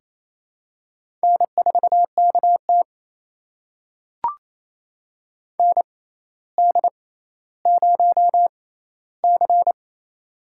Morse code, heard as a keyed tone, about 28 wpm, spaced as for 14 wpm, 700 Hz. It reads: N4KT ND0C
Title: USA Call Signs - 4 Characters - 6x Character Spacing - 28wpm